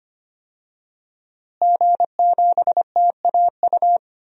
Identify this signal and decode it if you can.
Morse code, heard as a keyed tone, about 25 wpm, 700 Hz. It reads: G7TAU
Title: DXCC-100 Call Signs - 5 Characters - Repeated 2x - 25wpm